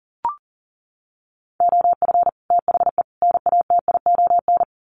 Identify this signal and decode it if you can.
Morse code, heard as a keyed tone, about 40 wpm, 700 Hz. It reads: OF THE NATION
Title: Sets of 3 Words 40wpm